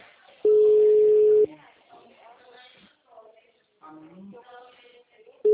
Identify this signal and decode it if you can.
Ringback tone (Europe, 425 Hz), with speech.